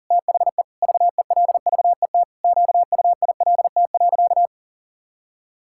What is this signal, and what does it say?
Morse code, heard as a keyed tone, about 40 wpm, 700 Hz. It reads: THE VELVET QUILT.
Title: Pangram Challenge v2 - 40wpm